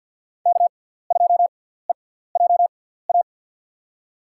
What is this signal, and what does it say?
Morse code, heard as a keyed tone, about 48 wpm, spaced as for 17 wpm, 700 Hz. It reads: K2EJA
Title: USA Call Signs - 5 Characters - 6x Character Spacing - 50wpm